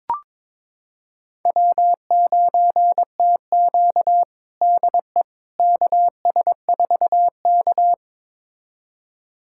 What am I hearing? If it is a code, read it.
Morse code, heard as a keyed tone, about 22 wpm, 700 Hz. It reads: W9TQ DE KS4K
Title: USA Callsign DE Callsign - 4 Characters - 22wpm